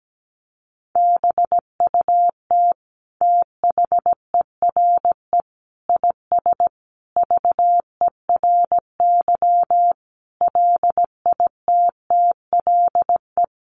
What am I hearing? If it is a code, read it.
Morse code, heard as a keyed tone, about 17 wpm, 700 Hz. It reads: BUT THERE IS VERY LITTLE
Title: Sentences from Top 1000 Words 17wpm